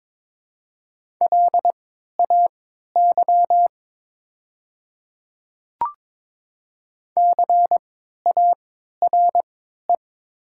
Morse code, heard as a keyed tone, about 23 wpm, spaced as for 13 wpm, 700 Hz. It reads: LAY CARE
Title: 3x Character Spacing - Top 500 Words - 22wpm